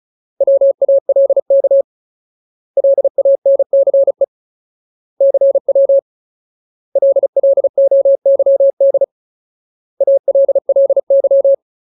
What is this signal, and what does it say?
Morse code, heard as a keyed tone, about 34 wpm, 550 Hz. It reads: WALK LANCE CW LLOYD ALLY
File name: Ultimate Word List / 35WPM Ultimate